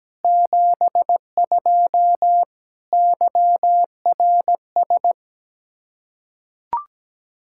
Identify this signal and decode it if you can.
Morse code, heard as a keyed tone, about 17 wpm, 700 Hz. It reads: 72 YRS
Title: K1USN SST QSO Practice 17wpm